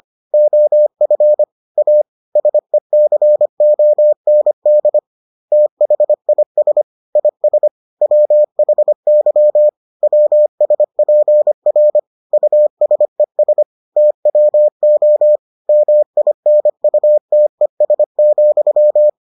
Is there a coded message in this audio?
Yes: Morse code from 0.3 to 19.1 s, heard as a keyed tone, about 25 wpm, 600 Hz. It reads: OF A SECOND THIS IS WHY WSPR USES TWO MINUTES,